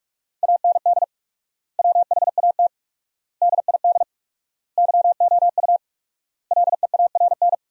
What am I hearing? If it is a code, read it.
Morse code, heard as a keyed tone, about 46 wpm, 700 Hz. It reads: AND WHAT DID YOU LEARN
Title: 4x Word-Spacing - Sentences from Top 300 Words - 45wpm